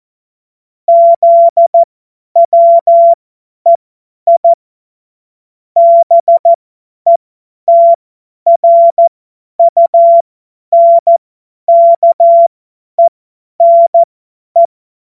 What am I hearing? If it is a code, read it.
Morse code, heard as a keyed tone, about 14 wpm, spaced as for 10 wpm, 680 Hz. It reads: ZWEI BETRUNKENE
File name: Morsekurs / Witze / Witz 13. Juli 2021, 50 F70